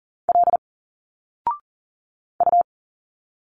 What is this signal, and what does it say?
Morse code, heard as a keyed tone, about 40 wpm, 700 Hz. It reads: L U
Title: Single Letters 40wpm